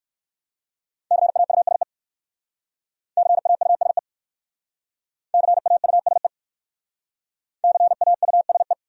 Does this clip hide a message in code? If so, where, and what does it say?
Yes: Morse code from 1.1 to 8.7 s, heard as a keyed tone, about 58 wpm, 700 Hz. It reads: CAUSE CAUSE CAUSE CAUSE